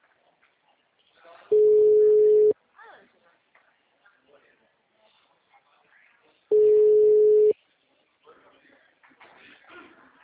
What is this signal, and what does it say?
Ringback tone (Europe, 425 Hz), with speech.
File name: out- -2011- - - .12